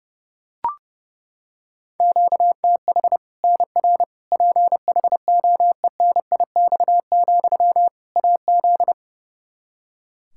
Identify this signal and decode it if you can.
Morse code, heard as a keyed tone, about 32 wpm, 700 Hz. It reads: QTH NR PHOENIX, AZ
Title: Practice QSO - Exchange 1 - Beginning - 30wpm